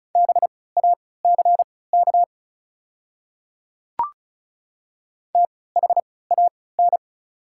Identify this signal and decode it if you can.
Morse code, heard as a keyed tone, about 36 wpm, spaced as for 20 wpm, 700 Hz. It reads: BACK THAN
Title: 3x Character Spacing - Top 300 Words - 35wpm